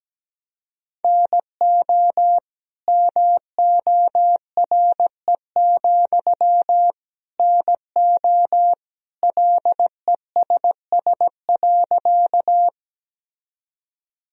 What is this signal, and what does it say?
Morse code, heard as a keyed tone, about 17 wpm, 700 Hz. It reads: NO MORE, NO LESS.
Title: Sentences from Top 1000 Words - Review Part 2 - 17wpm